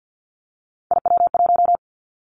Morse code, heard as a keyed tone, about 50 wpm, 700 Hz. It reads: IW1
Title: Callsign Prefixes for Top 50 DXCC - 50wpm